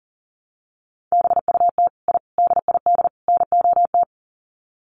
Morse code, heard as a keyed tone, about 40 wpm, 700 Hz. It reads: BUT I DID NOT
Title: Sets of 4 Words 40wpm